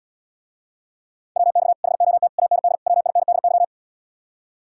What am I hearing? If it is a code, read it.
Morse code, heard as a keyed tone, about 80 wpm, 700 Hz. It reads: WX HOT AND WINDY